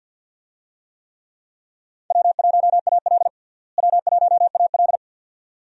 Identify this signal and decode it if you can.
Morse code, heard as a keyed tone, about 49 wpm, 700 Hz. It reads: W1AL W1AL
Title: USA Call Signs - 4 Characters - Repeated 2x - 50wpm